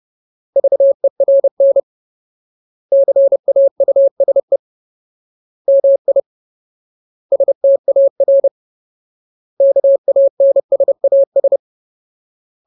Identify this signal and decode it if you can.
Morse code, heard as a keyed tone, about 30 wpm, 550 Hz. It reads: VERN CAUSE MI STAR KANSAS
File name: Ultimate Word List / 30WPM Ultimate